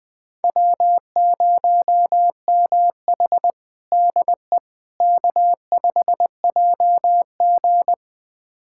Morse code, heard as a keyed tone, about 20 wpm, 700 Hz. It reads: W0MH DE K5JG
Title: USA Callsign DE Callsign - 4 Characters - 20wpm